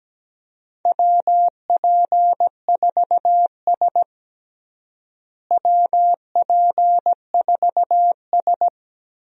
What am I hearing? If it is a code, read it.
Morse code, heard as a keyed tone, about 17 wpm, 700 Hz. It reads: WP4S WP4S
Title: USA Call Signs - 4 Characters - Repeated 3x - 17wpm